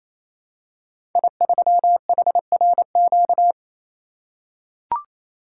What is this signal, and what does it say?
Morse code, heard as a keyed tone, about 28 wpm, 700 Hz. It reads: I3HRQ
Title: Code Groups - 5 Letter:Numbers - 28wpm